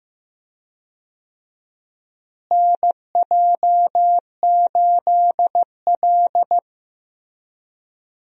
Morse code, heard as a keyed tone, about 16 wpm, 700 Hz. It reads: NJ8L
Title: USA Call Signs - 4 Characters - Repeated 3x - 15wpm